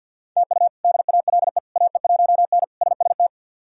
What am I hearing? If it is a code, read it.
Morse code, heard as a keyed tone, about 50 wpm, 700 Hz. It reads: TU DALE AE1N SST
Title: K1USN SST QSO Practice - 50wpm